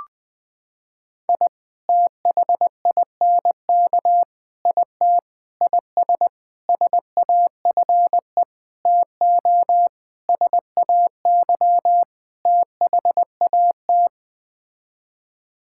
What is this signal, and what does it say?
Morse code, heard as a keyed tone, about 21 wpm, 700 Hz. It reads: I THINK IT IS SAFE TO SAY THAT
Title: Sets of 8 Words - 20wpm